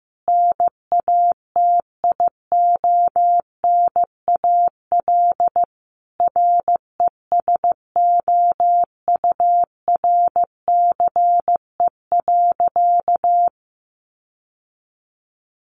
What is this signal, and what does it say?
Morse code, heard as a keyed tone, about 15 wpm, 700 Hz. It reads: NATIONAL RESOURCE.
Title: Sentences from Top 1300 Words 15wpm